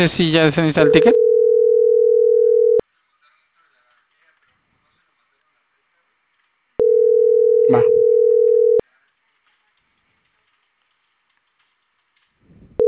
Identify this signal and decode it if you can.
Ringback tone (North America), 440 + 480 Hz, with speech.
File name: CENTRAL CAEA / out agcaea06